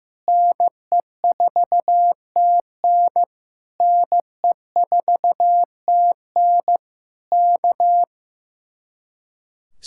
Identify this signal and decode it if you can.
Morse code, heard as a keyed tone, about 15 wpm, 700 Hz. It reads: NE4TN NE4TN K
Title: Speed-Racing Common QSO Phrases Encore - 15wpm